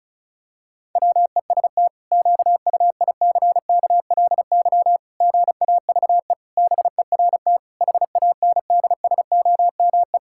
Morse code, heard as a keyed tone, about 36 wpm, 700 Hz. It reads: WEST QUICKLY GAVE BERT HANDSOME
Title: Pangram Challenge v2 - 35wpm